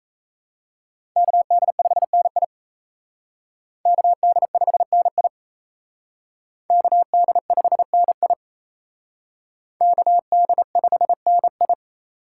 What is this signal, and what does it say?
Morse code, heard as a keyed tone, about 35 wpm, 700 Hz. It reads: KD5NI KD5NI KD5NI KD5NI
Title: Speed Racing - USA 2x2 Call Signs 28wpm